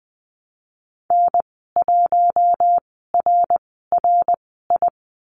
Morse code, heard as a keyed tone, about 20 wpm, spaced as for 15 wpm, 700 Hz. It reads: N1RRI
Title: 055 Lesson 10g - Letter r - Call Signs 20wpm